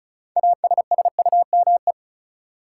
Morse code, heard as a keyed tone, about 36 wpm, 700 Hz. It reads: ASSUME
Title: Sentences from Top 1500 Words - New Word Drill - 35wpm